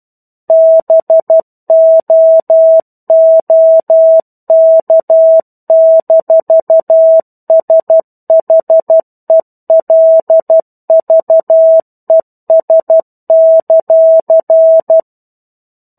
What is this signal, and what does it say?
Morse code, heard as a keyed tone, about 12 wpm, 650 Hz. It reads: BOOK-SHELVES;
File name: 01-Chapter-01 12 WPM